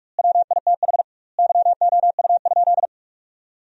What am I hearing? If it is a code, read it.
Morse code, heard as a keyed tone, about 46 wpm, 700 Hz. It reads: WITH YOU?
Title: 2x Word-Spacing - Sentences from Top 500 Words - 45wpm